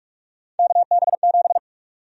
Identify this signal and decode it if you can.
Morse code, heard as a keyed tone, about 46 wpm, 700 Hz. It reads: KB7
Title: USA Call Sign Prefixes - 45wpm